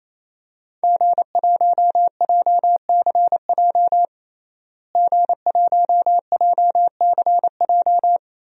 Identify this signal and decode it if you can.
Morse code, heard as a keyed tone, about 28 wpm, 700 Hz. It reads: G1JCJ G1JCJ
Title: DXCC-100 Call Signs - 5 Characters - Repeated 3x - 28wpm